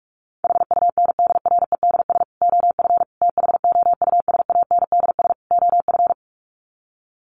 Morse code, heard as a keyed tone, about 44 wpm, 700 Hz. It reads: HUNDREDS OF THOUSANDS OF
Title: Sets of 4 Words 45wpm